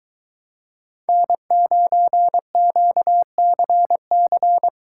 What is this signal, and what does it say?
Morse code, heard as a keyed tone, about 23 wpm, 700 Hz. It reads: N9QCC